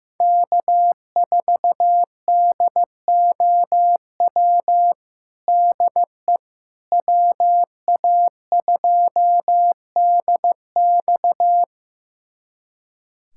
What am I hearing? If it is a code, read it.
Morse code, heard as a keyed tone, about 15 wpm, 700 Hz. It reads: K4DOW DE WA2DX
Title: USA Callsign DE Callsign - 5 Characters - 15wpm